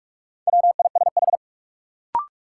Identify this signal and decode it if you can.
Morse code, heard as a keyed tone, about 42 wpm, 700 Hz. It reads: WISH
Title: Sentences from Top 1600 Words - New Word Drill Alt - 45wpm